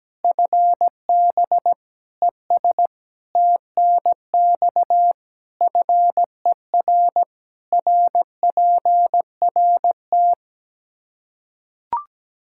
Morse code, heard as a keyed tone, about 17 wpm, 700 Hz. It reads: FB ES TNX FER RPRT
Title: K1USN SST QSO Practice 17wpm